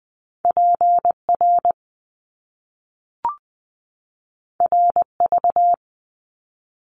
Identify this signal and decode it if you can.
Morse code, heard as a keyed tone, about 20 wpm, 700 Hz. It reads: PR RV
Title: Top Two Letter Combinations in Words 20wpm